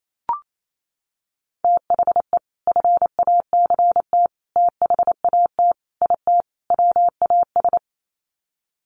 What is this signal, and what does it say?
Morse code, heard as a keyed tone, about 28 wpm, 700 Hz. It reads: THE FACT THAT IT WAS
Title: Sets of 5 Words Encore2 28wpm